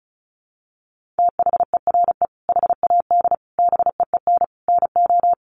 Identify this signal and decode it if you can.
Morse code, heard as a keyed tone, about 34 wpm, 700 Hz. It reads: THERE HAD BEEN NO
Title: Sentences from Top 1400 Words 35wpm